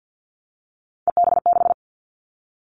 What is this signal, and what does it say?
Morse code, heard as a keyed tone, about 50 wpm, 700 Hz. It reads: EB6
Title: Callsign Prefixes for Top 100 DXCC - 50wpm